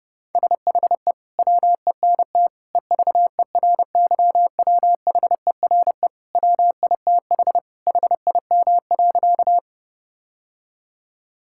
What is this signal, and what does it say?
Morse code, heard as a keyed tone, about 32 wpm, 700 Hz. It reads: SHE WENT EVERYWHERE WITH HIM.
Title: Sentences from Top 1600 Words 30wpm